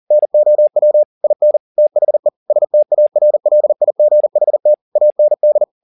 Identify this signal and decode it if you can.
Morse code, heard as a keyed tone, about 40 wpm, 600 Hz. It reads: NOW IN THE STARLIGHT AND
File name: the old man and the sea 2 40wpm